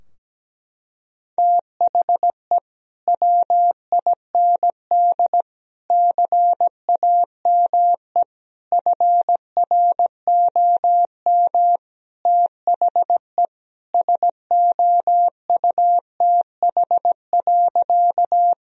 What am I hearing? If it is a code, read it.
Morse code, heard as a keyed tone, about 17 wpm, 700 Hz. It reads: THE WIND CAME FROM THE SOUTH.
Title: Sentences from Top 1000 Words - Review Part 2 - 17wpm